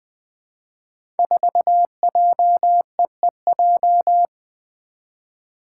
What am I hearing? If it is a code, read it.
Morse code, heard as a keyed tone, about 20 wpm, 700 Hz. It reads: 4JEEJ
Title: Five Challenging Letter-Number Combinations 20wpm